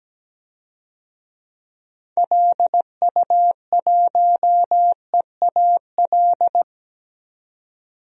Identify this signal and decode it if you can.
Morse code, heard as a keyed tone, about 17 wpm, 700 Hz. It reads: LU1EAL
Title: DXCC-100 Call Signs - 6 Characters - 17wpm